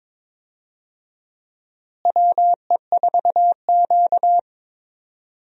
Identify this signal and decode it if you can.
Morse code, heard as a keyed tone, about 22 wpm, 700 Hz. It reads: WE4Q
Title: USA Call Signs - 4 Characters - Repeated 3x - 22wpm